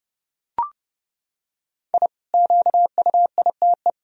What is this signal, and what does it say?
Morse code, heard as a keyed tone, about 30 wpm, 700 Hz. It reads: I QUITE
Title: Sentences from Top 1000 Words - Review Part 1 - 30wpm